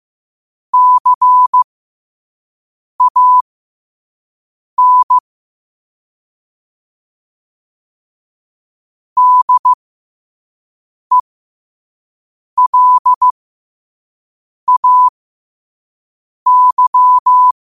Morse code, heard as a keyed tone, about 15 wpm, spaced as for 5 wpm, 1000 Hz. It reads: CAN DELAY
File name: qotdfiles / 5wpm